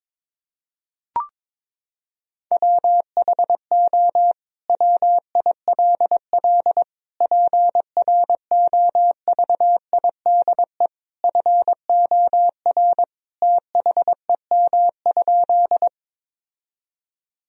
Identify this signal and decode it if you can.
Morse code, heard as a keyed tone, about 22 wpm, 700 Hz. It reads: WHO WILL PROVIDE FOR THEM?
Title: Sentences from Top 1000 Words - Review Part 3 - 22wpm